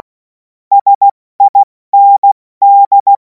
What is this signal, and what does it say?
Morse code, heard as a keyed tone, about 16 wpm, 800 Hz. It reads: SIND
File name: Morsekurs / Witze / Witz 02. Juni 2026 70 F80